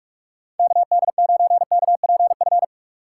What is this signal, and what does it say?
Morse code, heard as a keyed tone, about 44 wpm, 700 Hz. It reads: KD9KPF
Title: USA Call Signs - 6 Characters - 6x Character Spacing - 45wpm